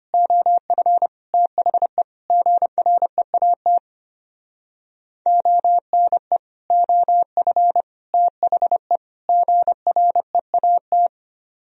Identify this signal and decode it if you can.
Morse code, heard as a keyed tone, about 31 wpm, 700 Hz. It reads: OF THE GREAT ONE OF THE GREAT
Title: Speed-Racing - Sets of 4 Words 25wpm